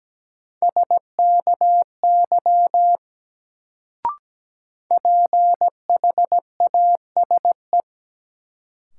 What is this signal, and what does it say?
Morse code, heard as a keyed tone, about 17 wpm, 700 Hz. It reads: SKY PHASE
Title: Sentences from Top 1200 Words - New Word Drill - 17wpm